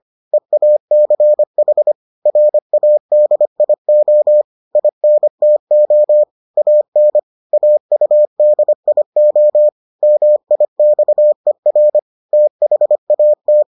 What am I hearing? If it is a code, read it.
Morse code, heard as a keyed tone, about 25 wpm, 600 Hz. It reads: EACH RADIO INTO AN AUDIO MIXER THAT